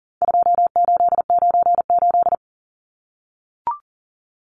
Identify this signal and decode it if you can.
Morse code, heard as a keyed tone, about 40 wpm, 700 Hz. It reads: 2898
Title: Numbers - 4-Digit 40wpm